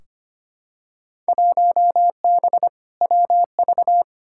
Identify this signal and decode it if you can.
Morse code, heard as a keyed tone, about 25 wpm, 700 Hz. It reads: 1B WV
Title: Field Day Exchanges - 25wpm